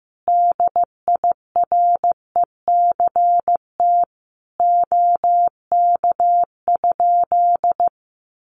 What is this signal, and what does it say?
Morse code, heard as a keyed tone, about 15 wpm, 700 Hz. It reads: DIRECT OK?
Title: Common QSO Phrases 15wpm